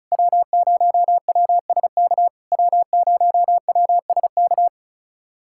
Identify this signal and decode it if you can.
Morse code, heard as a keyed tone, about 34 wpm, 700 Hz. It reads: W0WSK W0WSK
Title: SOTA QSO Practice 35wpm